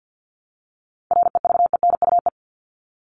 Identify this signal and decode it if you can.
Morse code, heard as a keyed tone, about 48 wpm, 700 Hz. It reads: REVENUE
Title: Top 1501-1750 Words 50wpm